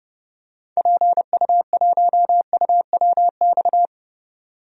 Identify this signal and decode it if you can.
Morse code, heard as a keyed tone, about 32 wpm, 700 Hz. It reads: PU1UWX
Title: DXCC-100 Call Signs - 6 Characters - 30wpm